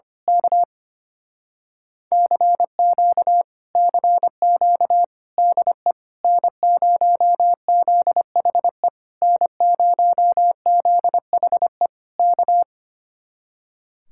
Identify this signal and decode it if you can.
Morse code, heard as a keyed tone, about 24 wpm, 700 Hz. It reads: K CQ CQ DE N0ZHE N0ZHE K
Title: Speed-Racing Common QSO Phrases - 25wpm